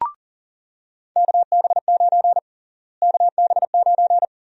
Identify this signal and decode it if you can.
Morse code, heard as a keyed tone, about 40 wpm, 700 Hz. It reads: KB9 KB9
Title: USA Call Sign Prefixes - Repeated 2x - 40wpm